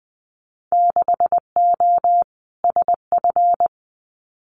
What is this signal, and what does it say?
Morse code, heard as a keyed tone, about 20 wpm, 700 Hz. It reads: 6O SF
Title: Winter Field Day Exchanges Encore 20wpm